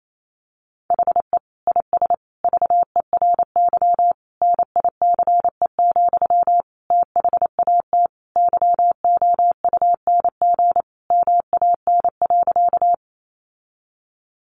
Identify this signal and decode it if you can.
Morse code, heard as a keyed tone, about 28 wpm, 700 Hz. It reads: HE IS VERY NICE, THAT YOUNG MAN.
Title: Sentences from Top 1100 Words 28wpm